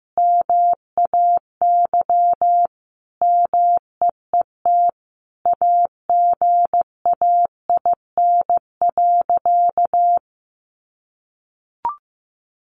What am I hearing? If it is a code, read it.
Morse code, heard as a keyed tone, about 15 wpm, 700 Hz. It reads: MAY MEET AGAIN.
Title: Sentences from Top 1100 Words 15wpm